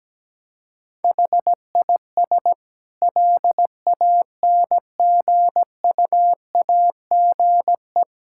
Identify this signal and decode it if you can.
Morse code, heard as a keyed tone, about 17 wpm, 700 Hz. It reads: HIS LANGUAGE
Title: Sentences from Top 1600 Words 17wpm